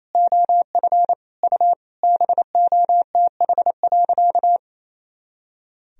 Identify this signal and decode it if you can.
Morse code, heard as a keyed tone, about 28 wpm, 700 Hz. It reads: OF U BOTH.
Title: Sentences from Top 300 Words and Abbreviations 28wpm